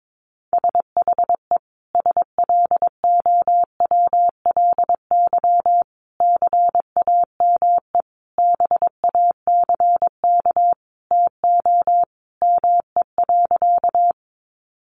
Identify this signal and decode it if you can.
Morse code, heard as a keyed tone, about 22 wpm, 700 Hz. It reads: SHE SLOWLY CAME BACK TO ME.
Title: Sentences from Top 1100 Words 22wpm